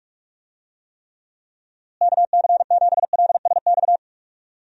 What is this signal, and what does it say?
Morse code, heard as a keyed tone, about 46 wpm, 700 Hz. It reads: KC7LSX
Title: USA Call Signs - 6 Characters - Repeated 3x - 45wpm